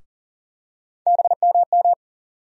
Morse code, heard as a keyed tone, about 40 wpm, 700 Hz. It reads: BMM
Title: USA Call Sign Postfixes - 3 Letters 40wpm